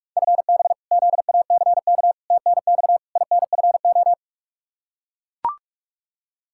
Morse code, heard as a keyed tone, about 42 wpm, 700 Hz. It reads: FB ZACK TNX INFO
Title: Practice QSO - Exchange 3 - More Info - 45wpm